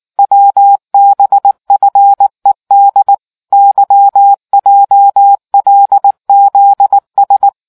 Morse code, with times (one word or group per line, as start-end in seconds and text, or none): 0.2-3.1 WBFED
3.5-7.5 YJLZS